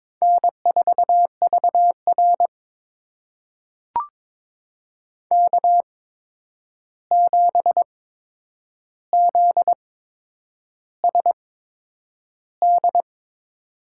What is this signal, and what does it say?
Morse code, heard as a keyed tone, about 22 wpm, spaced as for 9 wpm, 700 Hz. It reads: N4VR K7ZSD
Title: USA Call Signs - 8x Character Spacing - 22wpm